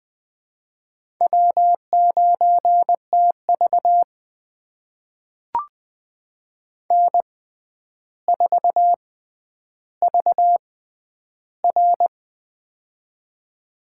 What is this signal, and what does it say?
Morse code, heard as a keyed tone, about 20 wpm, spaced as for 10 wpm, 700 Hz. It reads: W9TV N4VR
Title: USA Call Signs - 4 Characters - 6x Character Spacing - 20wpm